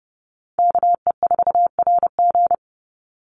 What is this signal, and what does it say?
Morse code, heard as a keyed tone, about 30 wpm, 700 Hz. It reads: KE4RG
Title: Call Signs - 30wpm